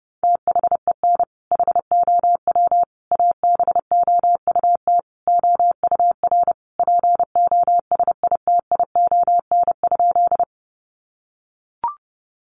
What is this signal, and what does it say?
Morse code, heard as a keyed tone, about 30 wpm, 700 Hz. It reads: THEN HOW ABOUT OUR POSITION?
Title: Sentences from Top 900 Words 30wpm